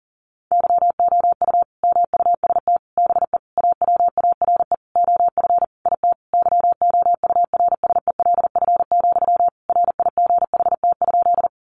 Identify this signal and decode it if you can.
Morse code, heard as a keyed tone, about 40 wpm, 700 Hz. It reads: YOU MUST BE AWARE OF IT YOURSELF, RIGHT?
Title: Sentences from Top 1200 Words 40wpm